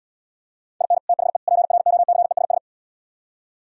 Morse code, heard as a keyed tone, about 77 wpm, 700 Hz. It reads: IN THE PROCESS